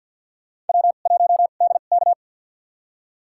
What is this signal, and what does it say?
Morse code, heard as a keyed tone, about 53 wpm, spaced as for 37 wpm, 700 Hz. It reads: W1DK
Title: USA Call Signs - 2x Character Spacing - 50wpm